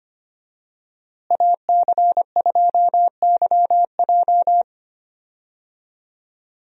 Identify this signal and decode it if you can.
Morse code, heard as a keyed tone, about 25 wpm, 700 Hz. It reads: AC2YJ
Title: USA Call Signs - 5 Characters - Repeated 2x - 25wpm